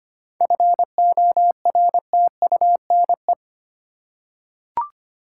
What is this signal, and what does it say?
Morse code, heard as a keyed tone, about 25 wpm, 700 Hz. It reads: FORTUNE
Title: Seven Letter Words - 25wpm